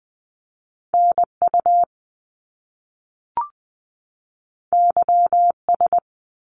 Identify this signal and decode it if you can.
Morse code, heard as a keyed tone, about 20 wpm, 700 Hz. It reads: NU YS
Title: Top Two Letter Combinations in Words 20wpm